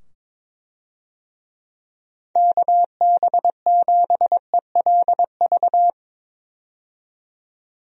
Morse code, heard as a keyed tone, about 22 wpm, 700 Hz. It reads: KB7ELV
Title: USA Call Signs - 6 Characters - 22wpm